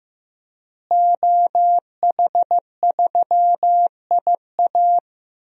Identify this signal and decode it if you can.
Morse code, heard as a keyed tone, about 16 wpm, 700 Hz. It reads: OH3IA
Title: DXCC-100 Call Signs - 5 Characters - Repeated 3x - 15wpm